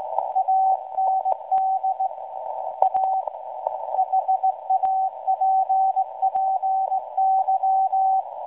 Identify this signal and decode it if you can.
Morse code, heard as a keyed tone, about 17 wpm, 760 Hz. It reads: AND A HAPPY
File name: WebSDR records / SAQ at PI4THT msg 0800-0804 17.2kHz